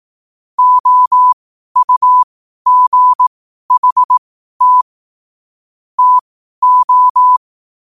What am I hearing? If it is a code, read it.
Morse code, heard as a keyed tone, about 18 wpm, spaced as for 12 wpm, 1000 Hz. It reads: OUGHT TO
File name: qotdfiles / 12wpm